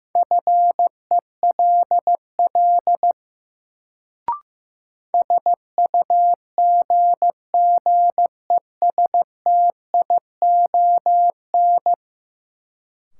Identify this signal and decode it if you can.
Morse code, heard as a keyed tone, about 15 wpm, 700 Hz. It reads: FELL SUGGESTION
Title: Sentences from Top 1600 Words - New Word Drill Alt - 15wpm